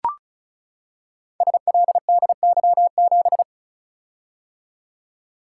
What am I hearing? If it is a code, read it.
Morse code, heard as a keyed tone, about 36 wpm, 700 Hz. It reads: SLDY7
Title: Code Groups - 5 Letter:Numbers - 35wpm